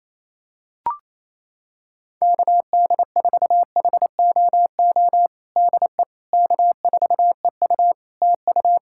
Morse code, heard as a keyed tone, about 27 wpm, 700 Hz. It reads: KD4HOO DE K4EU TU
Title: Practice QSO - Exchange 4 - Ending - 28wpm